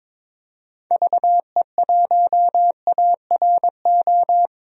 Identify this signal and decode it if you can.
Morse code, heard as a keyed tone, about 22 wpm, 700 Hz. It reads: VE1ARO
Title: DXCC-100 Call Signs - 6 Characters - 22wpm